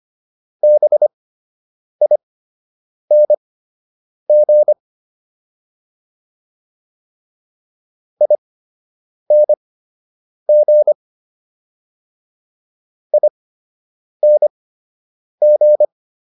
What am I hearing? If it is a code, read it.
Morse code, heard as a keyed tone, about 25 wpm, spaced as for 8 wpm, 600 Hz. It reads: BING ING ING